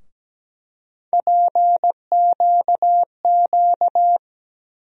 Morse code, heard as a keyed tone, about 17 wpm, 700 Hz. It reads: PQQ
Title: USA Call Sign Postfixes - 3 Letters 17wpm